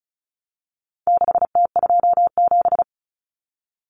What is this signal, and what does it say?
Morse code, heard as a keyed tone, about 34 wpm, 700 Hz. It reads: 6T27
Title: Common Cut Numbers - 4-Digit 35wpm